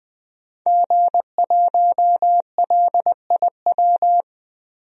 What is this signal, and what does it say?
Morse code, heard as a keyed tone, about 21 wpm, 700 Hz. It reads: G1LIW
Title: DXCC-100 Call Signs - 5 Characters - 20wpm